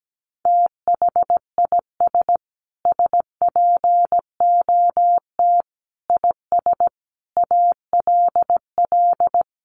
Morse code, heard as a keyed tone, about 17 wpm, 700 Hz. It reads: THIS SPOT IS ALL
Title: Sentences from Top 1100 Words 17wpm